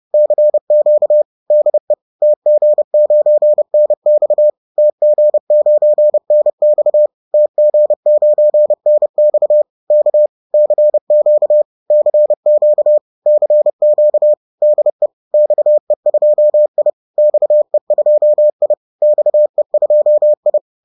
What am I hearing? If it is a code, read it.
Morse code, heard as a keyed tone, about 30 wpm, 600 Hz. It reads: CQ DE TG9NX TG9NX TG9NX K CQ CQ CQ DE XE2I XE2I XE2I